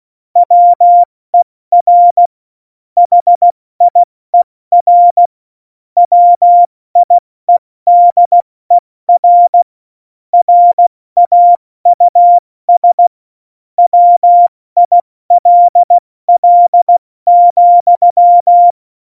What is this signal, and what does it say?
Morse code, heard as a keyed tone, about 16 wpm, spaced as for 14 wpm, 700 Hz. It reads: WER HIER WIEDER RAUS WILL,
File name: Morsekurs / Witze / Witz 22. Dezember 2020, 70 F80